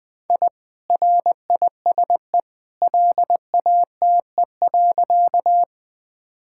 Morse code, heard as a keyed tone, about 20 wpm, 700 Hz. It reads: I RISE LATE.
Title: Sentences from Top 1000 Words - Review Part 6 - 20wpm